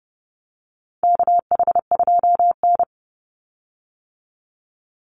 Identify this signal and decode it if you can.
Morse code, heard as a keyed tone, about 30 wpm, 700 Hz. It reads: KH2N
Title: Call Signs - 30wpm